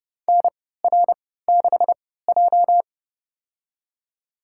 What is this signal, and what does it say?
Morse code, heard as a keyed tone, about 30 wpm, spaced as for 17 wpm, 700 Hz. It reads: NR6J
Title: USA Call Signs - 4 Characters - 3x Character Spacing - 30wpm